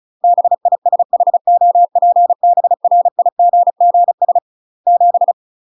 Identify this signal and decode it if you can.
Morse code, heard as a keyed tone, about 36 wpm, 700 Hz. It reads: BISHOPBRIGGS 7
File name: UK MASTER QSOs TRAINING / 35WPM / UK MASTER QSOs 0031